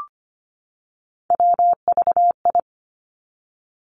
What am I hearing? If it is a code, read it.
Morse code, heard as a keyed tone, about 25 wpm, 700 Hz. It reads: WVI
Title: Three Challenging Letter-Number Combinations 25wpm